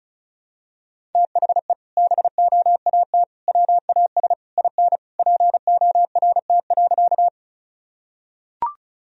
Morse code, heard as a keyed tone, about 34 wpm, 700 Hz. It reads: THE BOAT WAS IN PORT.
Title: Sentences from Top 1000 Words - Review Part 1 - 35wpm